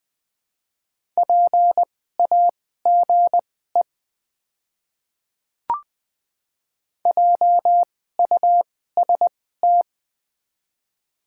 Morse code, heard as a keyed tone, about 21 wpm, spaced as for 15 wpm, 700 Hz. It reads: PAGE JUST
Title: 2x Character Spacing - Top 300 Words - 20wpm